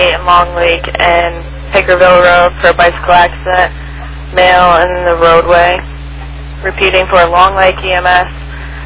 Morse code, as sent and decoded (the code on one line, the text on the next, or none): none